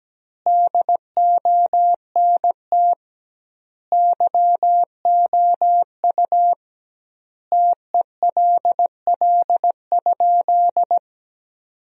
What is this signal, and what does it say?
Morse code, heard as a keyed tone, about 17 wpm, 700 Hz. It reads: DONT YOU TELL?
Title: 2x Word-Spacing - Sentences from Top 300 Words - 17wpm